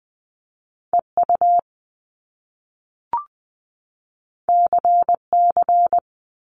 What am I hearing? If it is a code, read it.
Morse code, heard as a keyed tone, about 20 wpm, 700 Hz. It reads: EU CC
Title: Top Two Letter Combinations in Words 20wpm